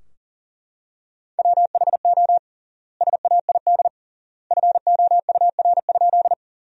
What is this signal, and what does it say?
Morse code, heard as a keyed tone, about 40 wpm, 700 Hz. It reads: WHO SAID FOUR?
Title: 3x Word-Spacing - Sentences from Top 300 Words - 40wpm